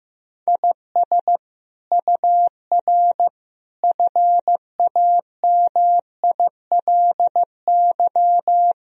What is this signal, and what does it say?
Morse code, heard as a keyed tone, about 16 wpm, 700 Hz. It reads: IS UR FAMILY
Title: Sentences from Top 400 Words and Abbreviations 15wpm